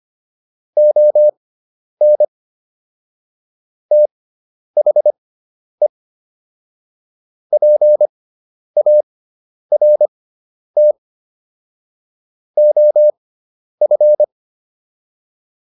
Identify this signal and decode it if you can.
Morse code, heard as a keyed tone, about 25 wpm, spaced as for 10 wpm, 600 Hz. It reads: ON THE PART OF
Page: intermediate-practice-files